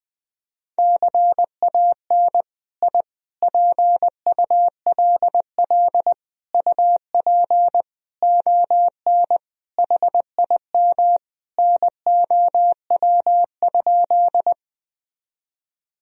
Morse code, heard as a keyed tone, about 20 wpm, 700 Hz. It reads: CAN I PULL UP ON HIM NOW?
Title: Sentences from Top 1000 Words - Review Part 2 - 20wpm